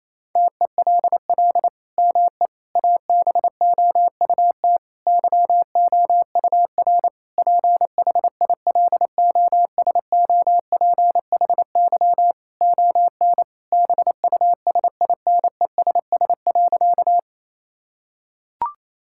Morse code, heard as a keyed tone, about 28 wpm, 700 Hz. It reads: TELL ME ABOUT YOUR PHILOSOPHY ON BUSINESS.
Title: Sentences from Top 1000 Words - Review Part 6 - 28wpm